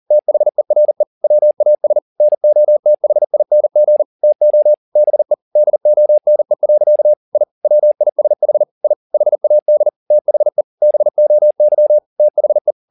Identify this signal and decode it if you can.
Morse code, heard as a keyed tone, about 40 wpm, 600 Hz. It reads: THERE WAS NOTHING TO BE DONE. I WISH I HAD THE BOY THE